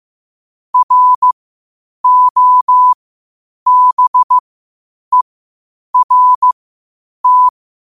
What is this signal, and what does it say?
Morse code, heard as a keyed tone, about 15 wpm, spaced as for 8 wpm, 1000 Hz. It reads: ROBERT
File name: qotdfiles / 8wpm